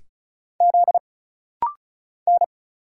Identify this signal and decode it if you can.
Morse code, heard as a keyed tone, about 36 wpm, 700 Hz. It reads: Z N
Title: Single Letter-Number Alt - Rapid-Fire 35wpm